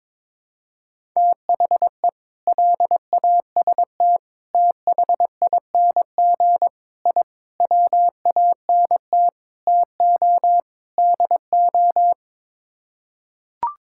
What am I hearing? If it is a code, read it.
Morse code, heard as a keyed tone, about 22 wpm, 700 Hz. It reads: THE LAST THING I WANT TO DO
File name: practice / 22wpm / Sets of 7 Words - 22wpm